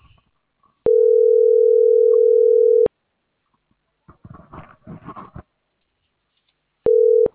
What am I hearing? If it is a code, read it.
Ringback tone (North America), 440 + 480 Hz.